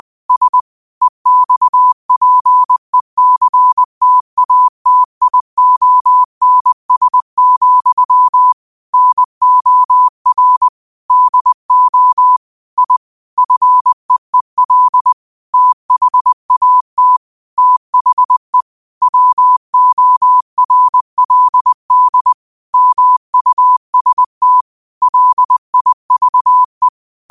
Morse code, heard as a keyed tone, about 20 wpm, 1000 Hz. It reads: S EXPECTATIONS, NOR DO I FEEL THAT THE WORLD MUST LIVE
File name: qotdfiles / 20wpm